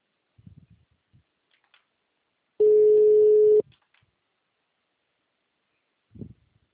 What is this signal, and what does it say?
Ringback tone (Europe, 425 Hz).